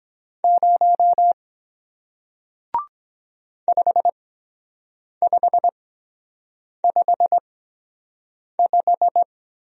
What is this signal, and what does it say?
Morse code, heard as a keyed tone, about 23 wpm, spaced as for 7 wpm, 700 Hz. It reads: 0 5555